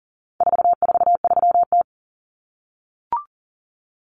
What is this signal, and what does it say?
Morse code, heard as a keyed tone, about 40 wpm, 700 Hz. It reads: 443T
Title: Common Cut Numbers - 4-Digit 40wpm